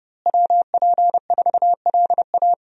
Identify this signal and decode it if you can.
Morse code, heard as a keyed tone, about 30 wpm, 700 Hz. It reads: WP4LA
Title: USA Call Signs - 5 Characters - Repeated 3x - 30wpm